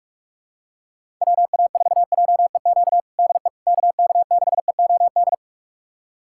Morse code, heard as a keyed tone, about 46 wpm, 700 Hz. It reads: WA4JEQ DE KK6EOD
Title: USA Callsign DE Callsign 45wpm